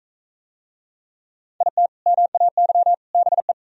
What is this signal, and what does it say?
Morse code, heard as a keyed tone, about 40 wpm, 700 Hz. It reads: IT MAY BE